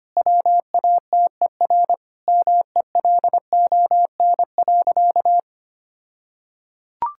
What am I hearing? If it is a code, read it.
Morse code, heard as a keyed tone, about 25 wpm, 700 Hz. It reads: WATER MELON.
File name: practice / 25wpm / All-Age Appropriate Jokes - 25wpm